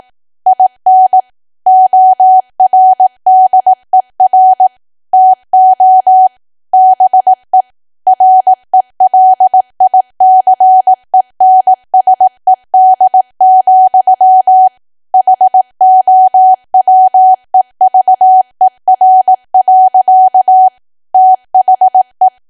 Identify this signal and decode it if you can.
Morse code, heard as a keyed tone, about 18 wpm, 750 Hz. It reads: IN ORDER TO BE RELICENSED, HOWEVER. THE
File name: cwa / W1AW-Level2 / 18WPM